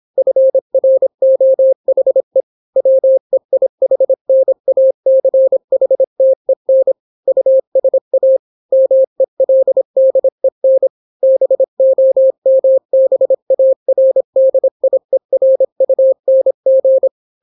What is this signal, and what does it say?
Morse code, heard as a keyed tone, about 26 wpm, 530 Hz. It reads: FROHE WEIHNACHTEN USA MELDEN BOMBARDIERUNG